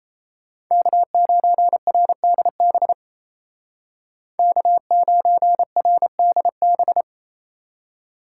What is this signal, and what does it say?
Morse code, heard as a keyed tone, about 36 wpm, 700 Hz. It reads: K9RDB K9RDB